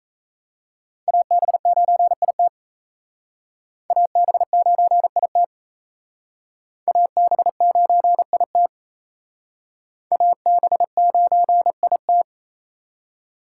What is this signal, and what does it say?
Morse code, heard as a keyed tone, about 35 wpm, 700 Hz. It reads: AB9IT AB9IT AB9IT AB9IT
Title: Speed Racing - USA 2x2 Call Signs 28wpm